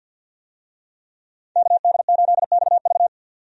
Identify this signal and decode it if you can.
Morse code, heard as a keyed tone, about 50 wpm, 700 Hz. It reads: KD8XV
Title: USA Call Signs - 5 Characters - 50wpm